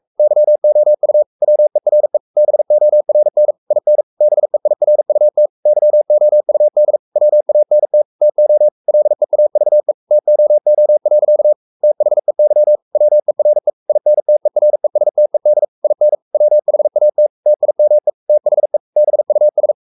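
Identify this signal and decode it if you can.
Morse code, heard as a keyed tone, about 44 wpm, 600 Hz. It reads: YOU WERE BORN IN BEIRUT YOUD WANT TO LEAVE TOO. THEY WERE INTERESTED IN WHAT TIME THE BUS
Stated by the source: Morse code at 43 wpm sending Ken Follett's novel Never in English, from CW med Gnister.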